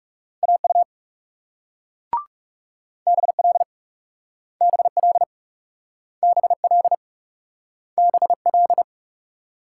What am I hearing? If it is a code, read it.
Morse code, heard as a keyed tone, about 41 wpm, 700 Hz. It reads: AU BL BL BL BL